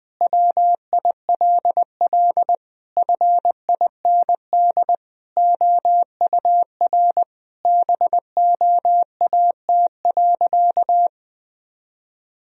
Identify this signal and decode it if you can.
Morse code, heard as a keyed tone, about 20 wpm, 700 Hz. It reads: WILL FIND OUR BOAT.
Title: Sentences from Top 1000 Words - Review Part 4 - 20wpm